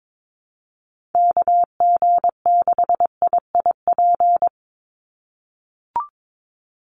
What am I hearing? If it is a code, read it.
Morse code, heard as a keyed tone, about 22 wpm, 700 Hz. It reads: KG6IIP
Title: Call Signs - 22wpm